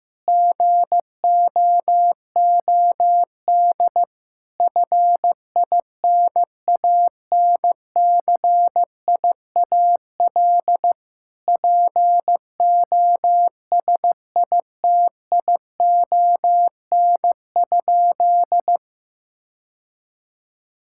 Morse code, heard as a keyed tone, about 16 wpm, 700 Hz. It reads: GOOD FINANCIAL POSITION?
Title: Sentences from Top 1000 Words - Review Part 5 - 15wpm